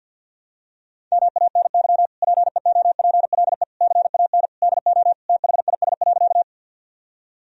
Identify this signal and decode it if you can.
Morse code, heard as a keyed tone, about 48 wpm, 700 Hz. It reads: MANY PEOPLE CAN DO THIS.
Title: Sentences from Top 1000 Words - Review Part 1 - 50wpm